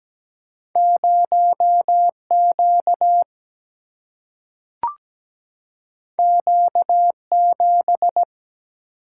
Morse code, heard as a keyed tone, about 17 wpm, 700 Hz. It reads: 0Q Q7
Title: Single-Letter Single-Number 17wpm